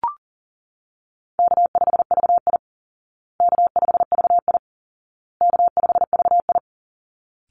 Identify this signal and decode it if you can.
Morse code, heard as a keyed tone, about 40 wpm, 700 Hz. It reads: K5VI K5VI K5VI
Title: Easy Call Signs Repeated 3x - 40wpm